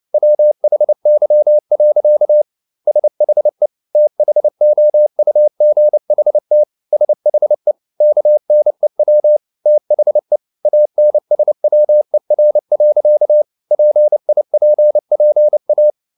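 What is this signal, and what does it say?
Morse code, heard as a keyed tone, about 29 wpm, 600 Hz. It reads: WHY. SHE THOUGHT SHE KNEW THE ANSWER. PIPPA
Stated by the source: Morse code at 29 wpm, sending Ken Follett's novel Never in English, from CW med Gnister.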